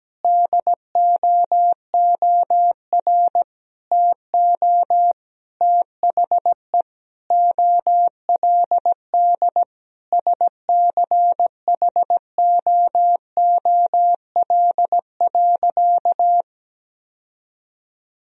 Morse code, heard as a keyed tone, about 17 wpm, 700 Hz. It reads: DOOR TO THE OLD SCHOOL.
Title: Sentences from Top 1000 Words - Review Part 1 - 17wpm